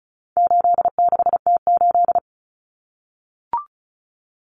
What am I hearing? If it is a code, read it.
Morse code, heard as a keyed tone, about 34 wpm, 700 Hz. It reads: 86T8
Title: Common Cut Numbers - 4-Digit 35wpm